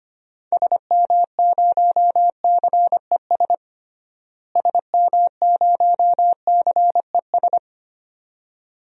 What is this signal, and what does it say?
Morse code, heard as a keyed tone, about 25 wpm, 700 Hz. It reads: SM0CES SM0CES
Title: DXCC-100 Call Signs - 6 Characters - Repeated 2x - 25wpm